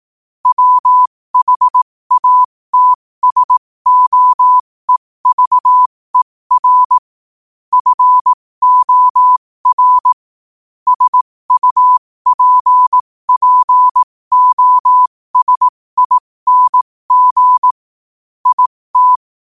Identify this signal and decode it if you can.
Morse code, heard as a keyed tone, about 18 wpm, spaced as for 15 wpm, 1000 Hz. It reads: WHATSOEVER FOR SUPPOSING IT
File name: qotdfiles / 15wpm